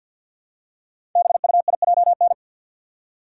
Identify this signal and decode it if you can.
Morse code, heard as a keyed tone, about 52 wpm, 700 Hz. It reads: BUIJN
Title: Code Groups - 5 Letters - 50wpm